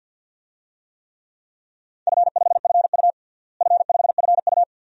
Morse code, heard as a keyed tone, about 51 wpm, 700 Hz. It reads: F5FU F5FU
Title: DXCC-100 Call Signs - 4 Characters - Repeated 2x - 50wpm